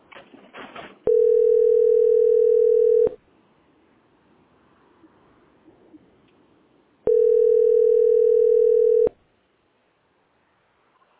Ringback tone (North America), 440 + 480 Hz.